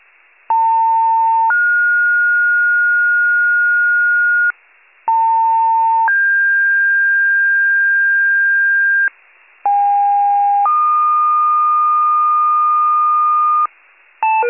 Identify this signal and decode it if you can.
Two-tone page, heard as a one-second tone, then a three-second tone, about 900 then 1430 Hz, about 900 then 1600 Hz, about 810 then 1220 Hz, with noise. It is heard from 0.5 to 13.7 s.
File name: wcalert